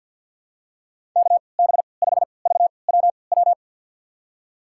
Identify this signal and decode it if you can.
Morse code, heard as a keyed tone, about 51 wpm, spaced as for 28 wpm, 700 Hz. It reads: KB5VWW
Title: USA Call Signs - 6 Characters - 3x Character Spacing - 50wpm